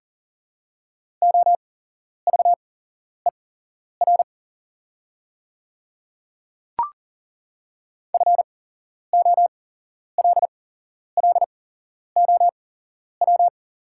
Morse code, heard as a keyed tone, about 41 wpm, spaced as for 11 wpm, 700 Hz. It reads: OVER FOLLOW
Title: 8x Character Spacing - Top 500 Words - 40wpm